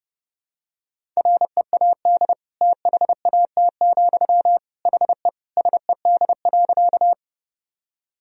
Morse code, heard as a keyed tone, about 30 wpm, 700 Hz. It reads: READ THAT, HE SED.
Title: Sentences from Top 200 Words and Abbreviations 30wpm